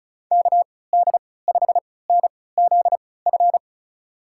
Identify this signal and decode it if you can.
Morse code, heard as a keyed tone, about 36 wpm, spaced as for 20 wpm, 700 Hz. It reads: KD5NZF
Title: USA Call Signs - 6 Characters - 3x Character Spacing - 35wpm